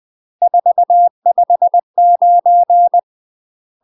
Morse code, heard as a keyed tone, about 20 wpm, 700 Hz. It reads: 459
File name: UK MASTER QSOs TRAINING/20WPM